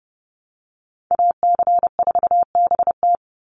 Morse code, heard as a keyed tone, about 30 wpm, 700 Hz. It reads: AC4BT
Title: Call Signs - 30wpm